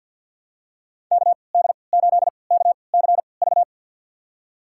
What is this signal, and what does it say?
Morse code, heard as a keyed tone, about 49 wpm, spaced as for 28 wpm, 700 Hz. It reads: KD8KCV
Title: USA Call Signs - 6 Characters - 3x Character Spacing - 50wpm